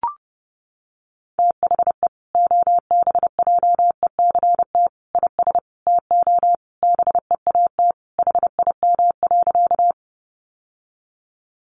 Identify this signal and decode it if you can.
Morse code, heard as a keyed tone, about 30 wpm, 700 Hz. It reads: THE OBJECT IS TO BEAT HIM.
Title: Sentences from Top 1300 Words 30wpm